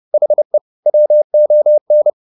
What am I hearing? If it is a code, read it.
Morse code, heard as a keyed tone, about 30 wpm, 600 Hz. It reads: HE WON
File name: the old man and the sea / the old man and the sea 1 30wpm